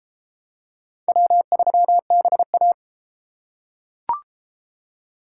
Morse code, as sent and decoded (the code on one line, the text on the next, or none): .-- ...-- -... .-
W3BA